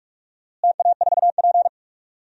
Morse code, heard as a keyed tone, about 46 wpm, 690 Hz. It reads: TA4P
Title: DXCC-100 Call Signs - 4 Characters - Repeated 3x - 45wpm